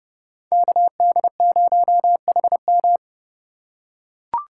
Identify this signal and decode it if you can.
Morse code, heard as a keyed tone, about 30 wpm, 700 Hz. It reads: KD0HM
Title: USA Call Signs - 5 Characters - 8x Character Spacing - 30wpm